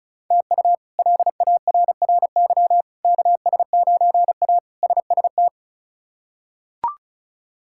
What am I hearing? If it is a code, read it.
Morse code, heard as a keyed tone, about 34 wpm, 700 Hz. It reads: TU LARRY KS9A SST
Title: K1USN SST QSO Practice - 35wpm